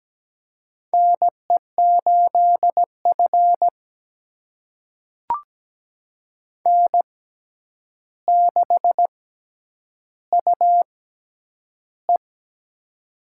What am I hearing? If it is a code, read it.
Morse code, heard as a keyed tone, about 17 wpm, spaced as for 9 wpm, 700 Hz. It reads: NE8F N6UE
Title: USA Call Signs - 4 Characters - 6x Character Spacing - 17wpm